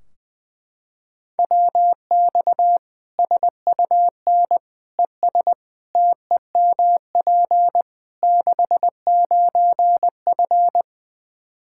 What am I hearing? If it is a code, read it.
Morse code, heard as a keyed tone, about 20 wpm, 700 Hz. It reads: WX SUN ES TEMP 69F
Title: K1USN SST QSO Practice 20wpm